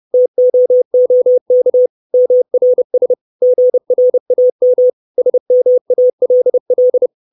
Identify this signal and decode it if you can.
Morse code, heard as a keyed tone, about 30 wpm, 500 Hz. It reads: TOOK MRS GRAM SMALL
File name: 632 words / 632 Words 30WPM